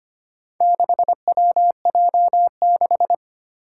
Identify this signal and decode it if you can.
Morse code, heard as a keyed tone, about 25 wpm, 700 Hz. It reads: 6WJ6
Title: Four Challenging Letter-Number Combinations 25wpm